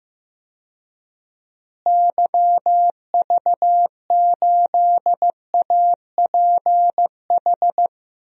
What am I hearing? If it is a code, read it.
Morse code, heard as a keyed tone, about 15 wpm, 700 Hz. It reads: YV8APH
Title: DXCC-100 Call Signs - 6 Characters - Repeated 3x - 15wpm